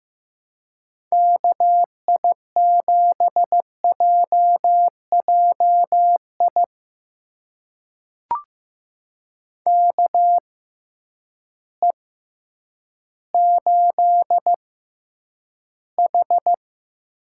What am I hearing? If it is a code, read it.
Morse code, heard as a keyed tone, about 15 wpm, spaced as for 9 wpm, 700 Hz. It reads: KI7JJI KE8H
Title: USA Call Signs - 6 Characters - 6x Character Spacing - 15wpm